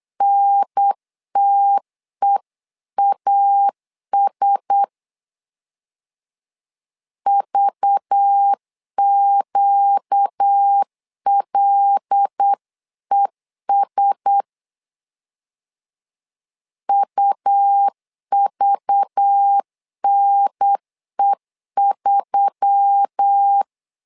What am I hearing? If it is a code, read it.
Morse code, heard as a keyed tone, about 8 wpm, 800 Hz. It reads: NTEAS VQLES UVNE3
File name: LEZIONI / 5 LBVC / 5 LBV 8wpm 2837